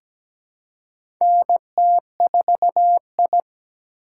Morse code, heard as a keyed tone, about 17 wpm, 700 Hz. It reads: NT4I
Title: USA Call Signs - 4 Characters - 2x Character Spacing - 17wpm